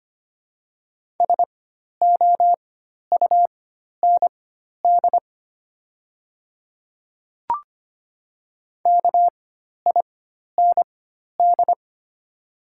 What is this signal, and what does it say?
Morse code, heard as a keyed tone, about 25 wpm, spaced as for 12 wpm, 700 Hz. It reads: SOUND KIND
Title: 4x Character Spacing - Top 300 Words - 25wpm